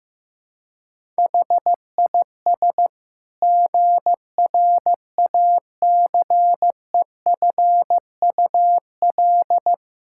Morse code, heard as a keyed tone, about 15 wpm, 700 Hz. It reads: HIS GRACEFUL
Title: Pangram Challenge v2 - 15wpm